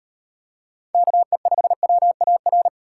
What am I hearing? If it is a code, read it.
Morse code, heard as a keyed tone, about 38 wpm, 700 Hz. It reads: KE5WAR